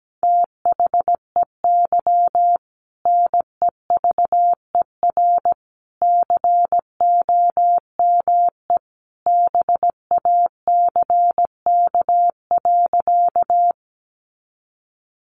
Morse code, heard as a keyed tone, about 17 wpm, 700 Hz. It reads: THEY NEVER COME BACK.
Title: Sentences from Top 300 Words 17wpm